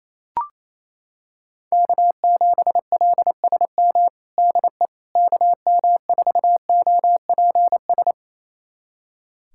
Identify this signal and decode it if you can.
Morse code, heard as a keyed tone, about 28 wpm, 700 Hz. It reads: K7LSM DE KM4OPS
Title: USA Callsign DE Callsign 28wpm